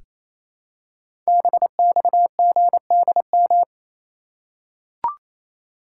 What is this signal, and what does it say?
Morse code, heard as a keyed tone, about 28 wpm, 700 Hz. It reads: BXGDM
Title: Code Groups - 5 Letters - 28wpm